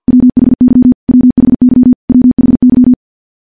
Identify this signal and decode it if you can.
Morse code, heard as a keyed tone, about 50 wpm, 250 Hz. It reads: WHY WHY WHY
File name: morse / morse-training-50WPM